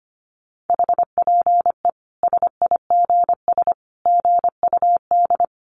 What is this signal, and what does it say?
Morse code, heard as a keyed tone, about 25 wpm, 700 Hz. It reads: HPE SIGS GUD
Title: Common QSO Phrases 25wpm